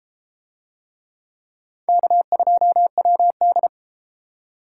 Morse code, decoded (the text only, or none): K2WD